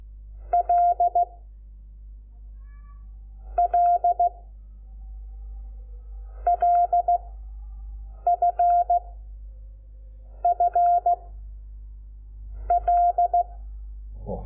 Morse code, heard as a keyed tone, about 16 wpm, 700 Hz. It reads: L L L F F L
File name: Lecciones / CLASE Nº5